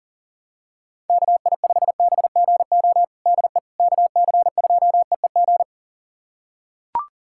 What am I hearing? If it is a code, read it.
Morse code, heard as a keyed tone, about 37 wpm, 700 Hz. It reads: KI5BGO DE KC2EEG
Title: USA Callsign DE Callsign 40wpm